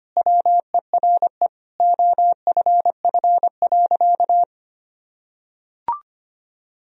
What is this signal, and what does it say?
Morse code, heard as a keyed tone, about 25 wpm, 700 Hz. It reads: WERE OFF.
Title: Sentences from Top 1000 Words - Review Part 1 - 25wpm